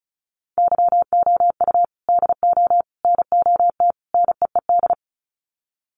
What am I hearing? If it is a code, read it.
Morse code, heard as a keyed tone, about 34 wpm, 700 Hz. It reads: YOU DO NOT NEED
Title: Sets of 4 Words 35wpm